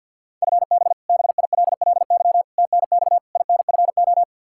Morse code, heard as a keyed tone, about 51 wpm, 700 Hz. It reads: FB BILLY TNX INFO
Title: K1USN SST QSO Practice 50wpm